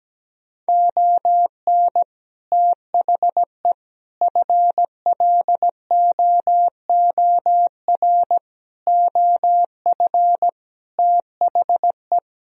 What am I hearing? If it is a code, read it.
Morse code, heard as a keyed tone, about 17 wpm, 700 Hz. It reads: ON THE FLOOR OF THE
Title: Speed-Racing - Sets of 5 Words 17wpm